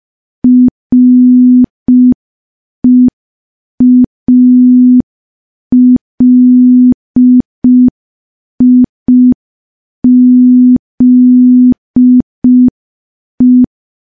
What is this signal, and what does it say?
Morse code, heard as a keyed tone, about 5 wpm, 250 Hz. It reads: REALIZE
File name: morse / morse-training-05WPM